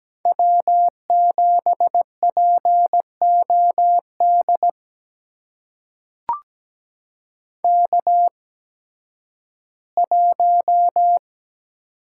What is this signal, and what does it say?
Morse code, heard as a keyed tone, about 17 wpm, spaced as for 11 wpm, 700 Hz. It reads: W7POD K1
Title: USA Call Signs - 8x Character Spacing - 17wpm